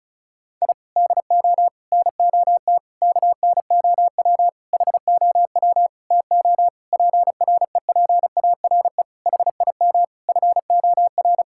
Morse code, heard as a keyed tone, about 34 wpm, 700 Hz. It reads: I DO NOT KNOW HOW TO PREPARE HIM FOR
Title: Sentences from Top 1000 Words - Review Part 6 - 35wpm